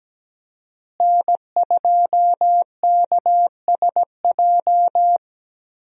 Morse code, heard as a keyed tone, about 17 wpm, 700 Hz. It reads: N2KSJ
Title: USA Call Signs - 5 Characters - Repeated 3x - 17wpm